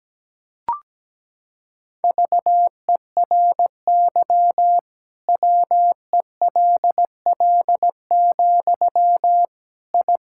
Morse code, heard as a keyed tone, about 17 wpm, 700 Hz. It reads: VERY WELL, I
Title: Sentences from Top 1000 Words - Review Part 5 - 17wpm